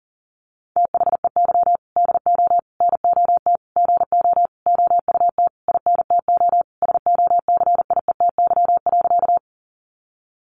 Morse code, heard as a keyed tone, about 40 wpm, 700 Hz. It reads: THEY DO NOT GO OUT INTO SOCIETY.
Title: Sentences from Top 700 Words 40wpm